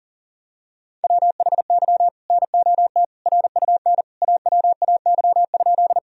Morse code, heard as a keyed tone, about 40 wpm, 700 Hz. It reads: WHY NOT RUN AWAY?
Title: Sentences from Top 1000 Words - Review Part 1 - 40wpm